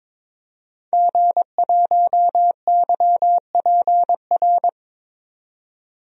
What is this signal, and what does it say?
Morse code, heard as a keyed tone, about 22 wpm, 700 Hz. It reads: G1YPR
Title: DXCC-100 Call Signs - 5 Characters - 22wpm